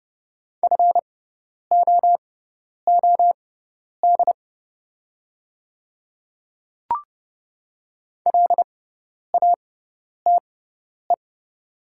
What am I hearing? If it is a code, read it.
Morse code, heard as a keyed tone, about 29 wpm, spaced as for 10 wpm, 700 Hz. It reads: FOOD LATE
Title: 6x Character Spacing - Top 300 Words - 30wpm